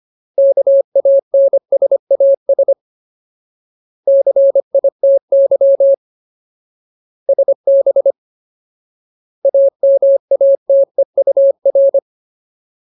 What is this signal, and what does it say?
Morse code, heard as a keyed tone, about 25 wpm, 550 Hz. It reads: KANSAS CITY SB AMATEUR
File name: Ultimate Word List / 25WPM Ultimate